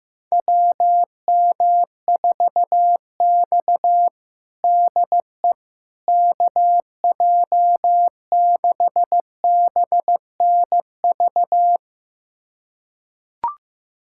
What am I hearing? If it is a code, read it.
Morse code, heard as a keyed tone, about 16 wpm, 700 Hz. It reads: WM4X DE KJ6BNV
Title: USA Callsign DE Callsign 15wpm